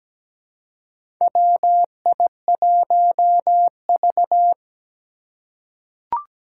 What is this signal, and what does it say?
Morse code, heard as a keyed tone, about 17 wpm, 700 Hz. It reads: WI1V
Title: USA Call Signs - 2x Character Spacing - 17wpm